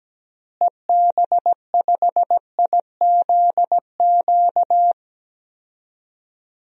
Morse code, heard as a keyed tone, about 17 wpm, 700 Hz. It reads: EB5IZQ
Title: DXCC-100 Call Signs - 6 Characters - Repeated 3x - 17wpm